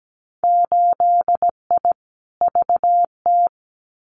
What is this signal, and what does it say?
Morse code, heard as a keyed tone, about 17 wpm, 700 Hz. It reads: 8I VT
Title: Winter Field Day Exchanges Encore 17wpm